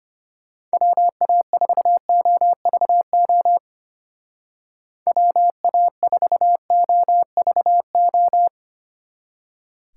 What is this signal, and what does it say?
Morse code, heard as a keyed tone, about 32 wpm, 700 Hz. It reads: WA4OVO WA4OVO